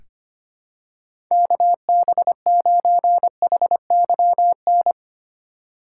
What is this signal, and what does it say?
Morse code, heard as a keyed tone, about 25 wpm, 700 Hz. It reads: KB9HYN
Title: USA Call Signs - 4x Character Spacing - 25wpm